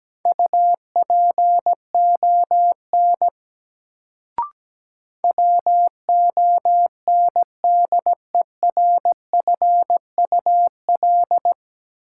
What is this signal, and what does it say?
Morse code, heard as a keyed tone, about 17 wpm, 700 Hz. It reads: UPON WONDERFUL
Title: Sentences from Top 1100 Words - New Word Drill Alt - 17wpm